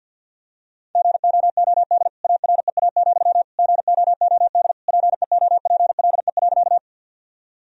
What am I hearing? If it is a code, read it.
Morse code, heard as a keyed tone, about 46 wpm, 700 Hz. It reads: GOOD AREA, GOOD PEOPLE.
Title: Sentences from Top 1000 Words - Review Part 2 - 50wpm